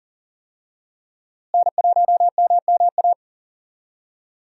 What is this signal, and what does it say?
Morse code, heard as a keyed tone, about 40 wpm, 700 Hz. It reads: N1MMA
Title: USA Call Signs - 5 Characters - Repeated 2x - 40wpm